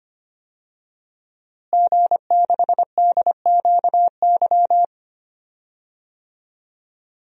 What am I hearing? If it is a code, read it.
Morse code, heard as a keyed tone, about 25 wpm, 700 Hz. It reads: G6DQY
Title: DXCC-100 Call Signs - 5 Characters - Repeated 2x - 25wpm